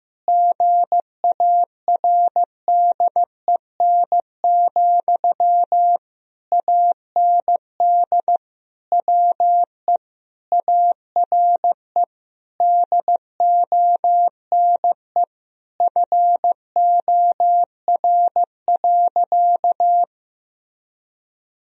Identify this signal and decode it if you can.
Morse code, heard as a keyed tone, about 15 wpm, 700 Hz. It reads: GARDEN, AND WE ARE DONE FOR.
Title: Sentences from Top 1000 Words - Review Part 4 - 15wpm